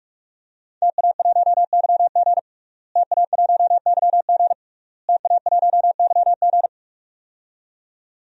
Morse code, heard as a keyed tone, about 46 wpm, 700 Hz. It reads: TA1YG TA1YG TA1YG
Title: DXCC-100 Call Signs - 5 Characters - Repeated 3x - 45wpm